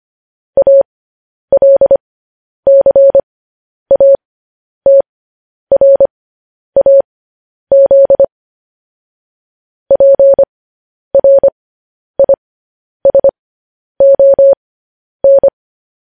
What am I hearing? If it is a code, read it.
Morse code, heard as a keyed tone, about 25 wpm, spaced as for 10 wpm, 550 Hz. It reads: ALCATRAZ PRISON
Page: news-headlines-in-morse-code-at-10-wpm